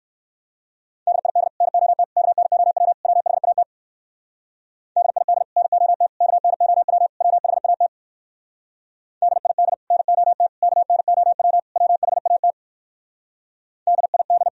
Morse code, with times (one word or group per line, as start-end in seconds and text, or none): 1.1-1.5 DID
1.6-2.0 NOT
2.2-2.9 KNOW
3.0-3.6 WHAT
5.0-5.4 DID
5.6-6.1 NOT
6.2-7.1 KNOW
7.2-7.9 WHAT
9.2-9.7 DID
9.9-10.5 NOT
10.6-11.6 KNOW
11.8-12.5 WHAT
13.9-14.5 DID